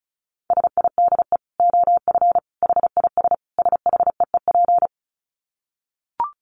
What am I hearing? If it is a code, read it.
Morse code, heard as a keyed tone, about 34 wpm, 700 Hz. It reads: SIDE OF HIS SHEEP
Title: Dit Challenge 35wpm